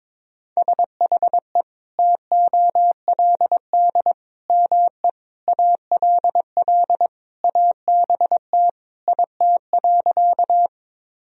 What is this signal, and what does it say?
Morse code, heard as a keyed tone, about 22 wpm, 700 Hz. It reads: SHE TOLD ME ALL ABT IT.
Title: Sentences from Top 400 Words and Abbreviations 22wpm